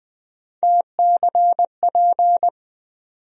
Morse code, heard as a keyed tone, about 20 wpm, 700 Hz. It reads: TCP
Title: USA Call Sign Postfixes - 3 Letters 20wpm